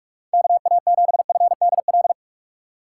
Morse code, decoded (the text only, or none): KA7FDL